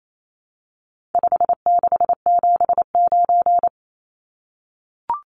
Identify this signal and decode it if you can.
Morse code, heard as a keyed tone, about 28 wpm, 700 Hz. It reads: 5679
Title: Numbers - 4-Digit 28wpm